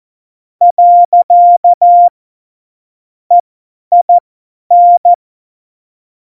Morse code, heard as a keyed tone, about 14 wpm, spaced as for 10 wpm, 700 Hz. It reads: . EIN
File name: Morsekurs / Witze / Witz 25. September 2018, 50 F70